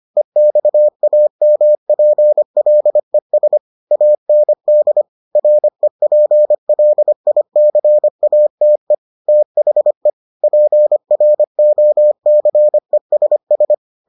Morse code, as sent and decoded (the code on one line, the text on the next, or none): . -..- .- -- .--. .-.. ... / .- -. -.. / .-. .--. .-.. .. -.-. .- - . / - .... / .--. .-. --- -.-. ... ...
EXAMPLES AND REPLICATE THE PROCESS